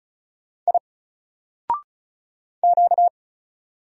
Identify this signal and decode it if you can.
Morse code, heard as a keyed tone, about 35 wpm, 700 Hz. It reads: I Q
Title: Single Letter-Number Alt 35wpm